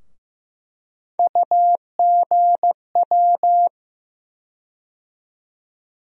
Morse code, heard as a keyed tone, about 15 wpm, 700 Hz. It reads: UGW
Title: USA Call Sign Postfixes - 3 Letters 15wpm